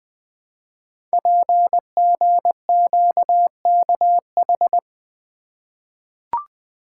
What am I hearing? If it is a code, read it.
Morse code, heard as a keyed tone, about 20 wpm, 700 Hz. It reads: PGQKH
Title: Code Groups - 5 Letters - 20wpm